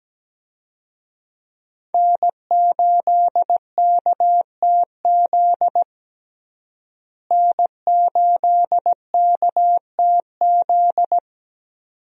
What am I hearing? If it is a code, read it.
Morse code, heard as a keyed tone, about 17 wpm, 700 Hz. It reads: N8KTZ N8KTZ
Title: USA Call Signs - 5 Characters - Repeated 2x - 17wpm